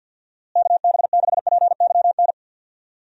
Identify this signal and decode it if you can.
Morse code, heard as a keyed tone, about 47 wpm, 700 Hz. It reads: KB6PYN
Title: USA Call Signs - 6 Characters - Repeated 2x - 50wpm